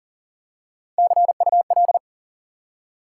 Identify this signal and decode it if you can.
Morse code, heard as a keyed tone, about 40 wpm, 700 Hz. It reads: CUL
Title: Top 90 QSO Elements 40wpm